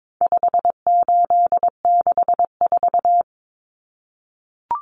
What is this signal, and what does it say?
Morse code, heard as a keyed tone, about 22 wpm, 700 Hz. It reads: 5864
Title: Numbers - 4-Digit 22wpm